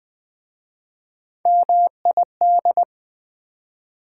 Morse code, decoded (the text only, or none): MID